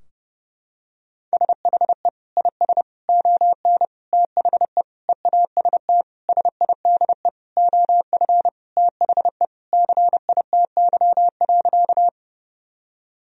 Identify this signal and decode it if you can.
Morse code, heard as a keyed tone, about 30 wpm, 700 Hz. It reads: SHE IS ON THE EAST SIDE OF THE CITY.
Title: Sentences from Top 1000 Words - Review Part 3 - 30wpm